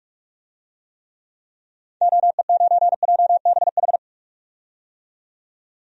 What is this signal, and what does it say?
Morse code, heard as a keyed tone, about 46 wpm, 700 Hz. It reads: OE9JBH
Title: DXCC-100 Call Signs - 6 Characters - 45wpm